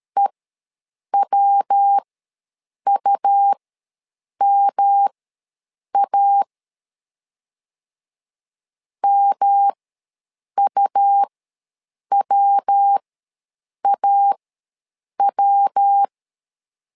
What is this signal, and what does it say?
Morse code, heard as a keyed tone, about 13 wpm, spaced as for 7 wpm, 800 Hz. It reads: EWUMA MUWAW
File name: LEZIONI / 3 MWRO / 3 MW ws 12wpm 1816